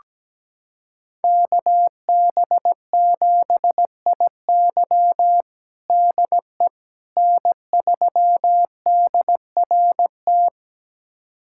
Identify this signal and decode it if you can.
Morse code, heard as a keyed tone, about 17 wpm, 700 Hz. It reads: KB7IY DE N3DRT
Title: USA Callsign DE Callsign - 5 Characters - 17wpm